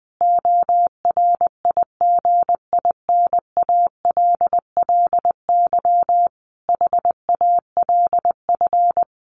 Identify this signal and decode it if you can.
Morse code, heard as a keyed tone, about 20 wpm, 700 Hz. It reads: ORIGINALLY HALF
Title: Sentences from Top 1500 Words 20wpm